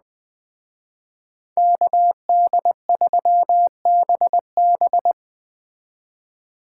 Morse code, heard as a keyed tone, about 21 wpm, 700 Hz. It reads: KD3BB